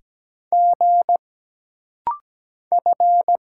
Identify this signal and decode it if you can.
Morse code, heard as a keyed tone, about 17 wpm, 700 Hz. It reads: G F
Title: Single Letter-Number Alt - Rapid-Fire 17wpm